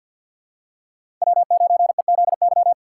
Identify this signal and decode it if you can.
Morse code, heard as a keyed tone, about 52 wpm, 700 Hz. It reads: W9EZY
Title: USA Call Signs - 5 Characters - 3x Character Spacing - 50wpm